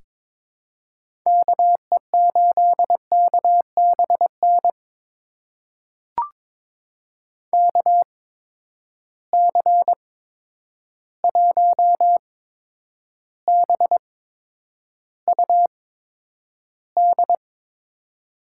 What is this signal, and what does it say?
Morse code, heard as a keyed tone, about 22 wpm, spaced as for 9 wpm, 700 Hz. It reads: KE8KBN KC1BUD
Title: USA Call Signs - 6 Characters - 8x Character Spacing - 22wpm